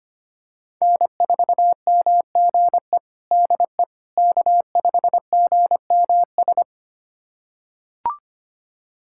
Morse code, heard as a keyed tone, about 25 wpm, 700 Hz. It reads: N4MGE DE K5GMS
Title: USA Callsign DE Callsign - 5 Characters - 25wpm